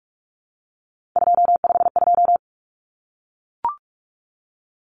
Morse code, heard as a keyed tone, about 44 wpm, 700 Hz. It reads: 252
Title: Common Cut Numbers - 3-Digit 45wpm